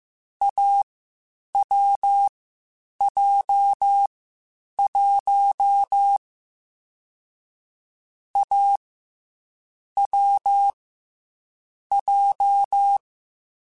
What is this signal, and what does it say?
Morse code, heard as a keyed tone, about 15 wpm, 800 Hz. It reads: A W J 1 A W J
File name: Difficult Combos / Difficult Combos-mp3s / AWJ1 @13